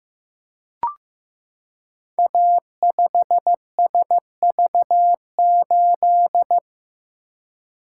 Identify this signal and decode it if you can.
Morse code, heard as a keyed tone, about 15 wpm, 700 Hz. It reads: A5SV8
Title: Code Groups - 5 Letter:Numbers - 15wpm